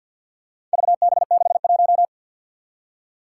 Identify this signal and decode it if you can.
Morse code, heard as a keyed tone, about 47 wpm, 700 Hz. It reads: VB61
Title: Four Challenging Letter-Number Combinations 50wpm